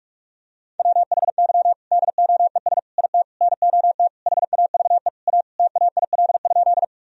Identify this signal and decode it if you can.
Morse code, heard as a keyed tone, about 41 wpm, 700 Hz. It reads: WHY DOES IT NOT HAVE A TAIL?
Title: Sentences from Top 1000 Words - Review Part 3 - 45wpm